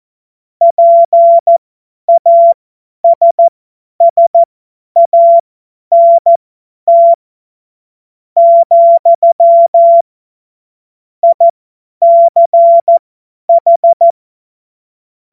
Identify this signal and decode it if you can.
Morse code, heard as a keyed tone, about 14 wpm, spaced as for 9 wpm, 680 Hz. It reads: PASSANT, ICH
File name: Witz-23-September-2025-50-F70